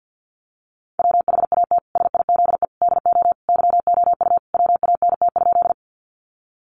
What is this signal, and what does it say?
Morse code, heard as a keyed tone, about 50 wpm, 700 Hz. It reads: WHAT SIZE DO YOU WANT?
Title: Sentences from Top 400 Words 50wpm